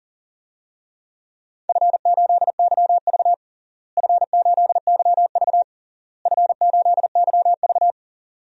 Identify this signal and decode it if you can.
Morse code, heard as a keyed tone, about 40 wpm, 700 Hz. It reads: F8YV F8YV F8YV
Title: DXCC-100 Call Signs - 4 Characters - Repeated 3x - 40wpm